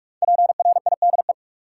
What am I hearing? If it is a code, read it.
Morse code, heard as a keyed tone, about 45 wpm, 700 Hz. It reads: PRIDE
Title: Five Letter Words - 45wpm